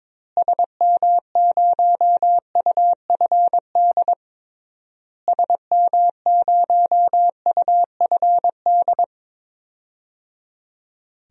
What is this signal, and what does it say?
Morse code, heard as a keyed tone, about 22 wpm, 700 Hz. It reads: SM0UFD SM0UFD
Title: DXCC-100 Call Signs - 6 Characters - Repeated 3x - 22wpm